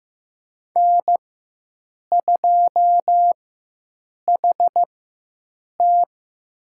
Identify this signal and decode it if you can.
Morse code, heard as a keyed tone, about 15 wpm, spaced as for 7 wpm, 700 Hz. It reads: N2HT
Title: USA Call Signs - 4x Character Spacing - 15wpm